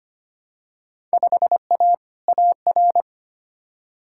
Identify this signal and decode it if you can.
Morse code, heard as a keyed tone, about 25 wpm, 700 Hz. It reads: 5A AR
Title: Field Day Exchanges Encore - 25wpm